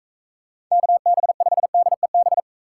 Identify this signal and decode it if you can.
Morse code, heard as a keyed tone, about 39 wpm, 700 Hz. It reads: KB5DEB